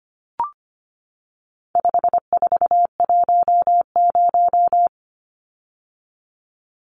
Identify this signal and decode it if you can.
Morse code, heard as a keyed tone, about 25 wpm, 700 Hz. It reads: 5410
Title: Numbers - 4-Digit 25wpm